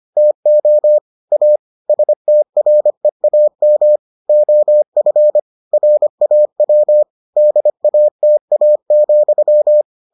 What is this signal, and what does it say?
Morse code, heard as a keyed tone, about 25 wpm, 600 Hz. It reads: TO A STREAM OF RAW DATA,